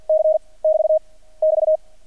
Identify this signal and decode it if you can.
Morse code, heard as a keyed tone, about 50 wpm, 630 Hz. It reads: C <BT> <BT>
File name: archived / CRSnet